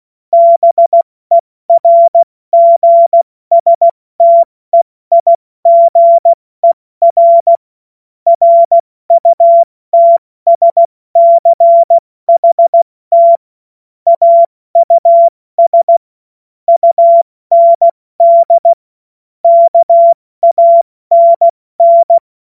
Morse code, heard as a keyed tone, about 16 wpm, spaced as for 14 wpm, 680 Hz. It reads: BERGSTEIGER RUTSCHT AUS UND KANN
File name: Morsekurs / Witze / Witz 11. Januar 2022, 70 F80